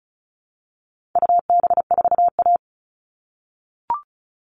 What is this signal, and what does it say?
Morse code, heard as a keyed tone, about 34 wpm, 700 Hz. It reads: UB4A
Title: All Cut Numbers - 4-Digit 35wpm